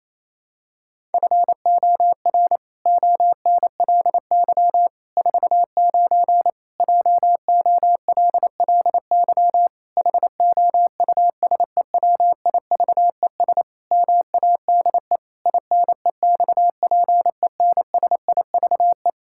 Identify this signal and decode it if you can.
Morse code, heard as a keyed tone, about 27 wpm, 700 Hz. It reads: FOR ONLY 49 JOLLY HOUSEWIVES MADE INEXPENSIVE
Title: Pangram Challenge v2 - 28wpm